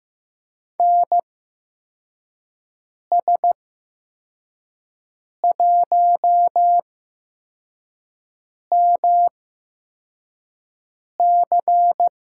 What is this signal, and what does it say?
Morse code, heard as a keyed tone, about 15 wpm, spaced as for 4 wpm, 700 Hz. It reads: NS1MC
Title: USA Call Signs - 5 Characters - 8x Character Spacing - 15wpm